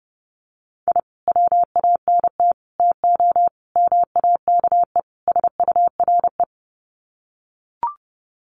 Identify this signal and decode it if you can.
Morse code, heard as a keyed tone, about 30 wpm, 700 Hz. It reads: I WANT TO MAKE SURE
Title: Sets of 5 Words Encore2 30wpm